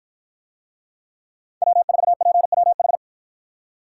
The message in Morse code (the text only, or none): W4PWH